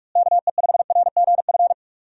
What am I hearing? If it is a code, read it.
Morse code, heard as a keyed tone, about 41 wpm, 700 Hz. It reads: KE5RGF
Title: USA Call Signs - 6 Characters - Repeated 2x - 45wpm